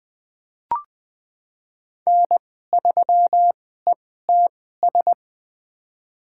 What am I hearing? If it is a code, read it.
Morse code, heard as a keyed tone, about 20 wpm, spaced as for 14 wpm, 700 Hz. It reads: N3ETS
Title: USA Call Signs - 5 Characters - 2x Character Spacing - 20wpm